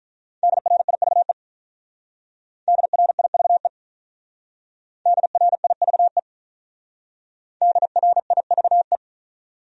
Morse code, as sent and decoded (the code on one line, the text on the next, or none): -.. .-. .. ...- . / -.. .-. .. ...- . / -.. .-. .. ...- . / -.. .-. .. ...- .
DRIVE DRIVE DRIVE DRIVE